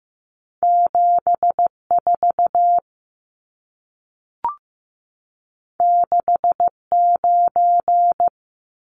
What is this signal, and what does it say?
Morse code, heard as a keyed tone, about 15 wpm, 700 Hz. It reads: 74 69
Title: Numbers - 2-Digit 15wpm